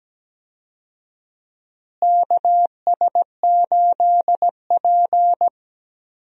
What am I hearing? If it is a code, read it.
Morse code, heard as a keyed tone, about 17 wpm, 700 Hz. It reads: KS8P
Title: USA Call Signs - 4 Characters - Repeated 3x - 17wpm